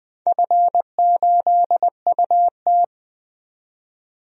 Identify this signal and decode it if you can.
Morse code, heard as a keyed tone, about 20 wpm, 700 Hz. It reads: F8UT
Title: DXCC-100 Call Signs - 4 Characters - 20wpm